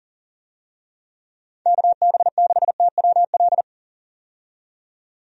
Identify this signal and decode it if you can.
Morse code, heard as a keyed tone, about 40 wpm, 700 Hz. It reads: KB6TWL
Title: USA Call Signs - 6 Characters - 40wpm